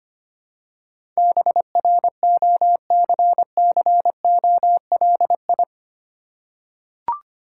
Morse code, heard as a keyed tone, about 25 wpm, 700 Hz. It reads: BROCCOLI
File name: practice / 25wpm / Difficult Words with Hints - 25wpm